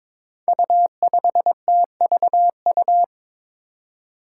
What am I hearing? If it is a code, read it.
Morse code, heard as a keyed tone, about 22 wpm, 700 Hz. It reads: U5TVU
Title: Five Challenging Letter-Number Combinations 22wpm